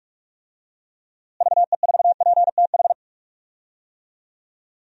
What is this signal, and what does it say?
Morse code, heard as a keyed tone, about 46 wpm, 700 Hz. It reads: VE4PTH
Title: DXCC-100 Call Signs - 6 Characters - Repeated 2x - 45wpm